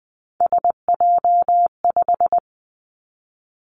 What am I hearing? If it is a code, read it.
Morse code, heard as a keyed tone, about 20 wpm, 700 Hz. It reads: SJ5
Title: Three Challenging Letter-Number Combinations 20wpm